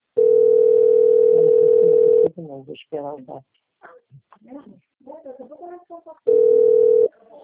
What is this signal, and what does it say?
Ringback tone (North America), 440 + 480 Hz, with speech.
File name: out- - - - - .30760